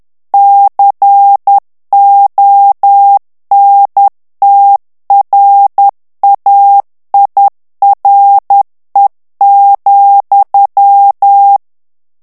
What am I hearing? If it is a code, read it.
Morse code, heard as a keyed tone, about 10 wpm, 800 Hz. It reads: CONTRAIRE,
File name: FRANCAIS TEXTE CLAIR 2 VITESSE 10 MOTS MINUTES